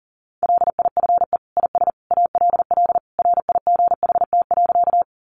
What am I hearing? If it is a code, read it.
Morse code, heard as a keyed tone, about 40 wpm, 700 Hz. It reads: LIFE IS ALL RIGHT.
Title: Sentences from Top 300 Words 40wpm